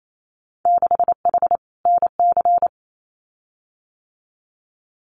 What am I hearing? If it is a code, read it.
Morse code, heard as a keyed tone, about 28 wpm, 700 Hz. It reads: 6H NC
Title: Winter Field Day Exchanges Encore 28wpm